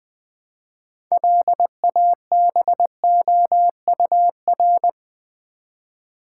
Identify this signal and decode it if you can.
Morse code, heard as a keyed tone, about 20 wpm, 700 Hz. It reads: LABOUR
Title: Build A Word - 2 Letters At A Time - 6-Letter Words - 20wpm